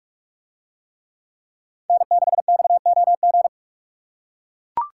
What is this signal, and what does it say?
Morse code, heard as a keyed tone, about 41 wpm, 700 Hz. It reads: N6XOG